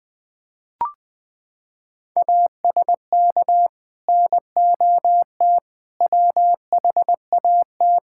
Morse code, heard as a keyed tone, about 20 wpm, 700 Hz. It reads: ASK NOT WHAT
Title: Sentences with 9-Word Phrases - 20wpm